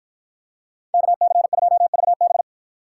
Morse code, heard as a keyed tone, about 53 wpm, 700 Hz. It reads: KC2VB